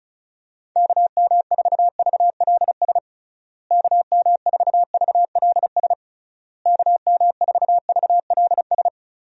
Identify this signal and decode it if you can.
Morse code, heard as a keyed tone, about 36 wpm, 700 Hz. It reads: KM4VLS KM4VLS KM4VLS
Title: USA Call Signs - 6 Characters - Repeated 3x - 35wpm